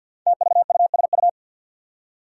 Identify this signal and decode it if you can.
Morse code, heard as a keyed tone, about 53 wpm, 700 Hz. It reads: TVUSU
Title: Five Challenging Letter-Number Combinations 50wpm